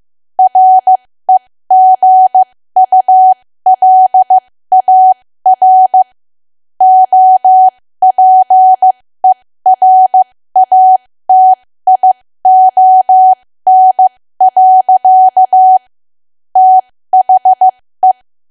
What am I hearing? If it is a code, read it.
Morse code, heard as a keyed tone, about 15 wpm, spaced as for 13 wpm, 750 Hz. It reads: REGULAR OPERATION. THE
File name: cwa / W1AW-Level2 / 13WPM